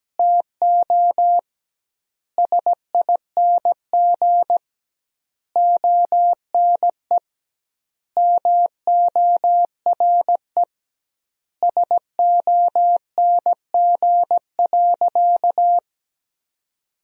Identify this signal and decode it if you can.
Morse code, heard as a keyed tone, about 17 wpm, 700 Hz. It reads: TO SING ONE MORE SONG.
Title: 2x Word-Spacing - Sentences from Top 400 Words - 17wpm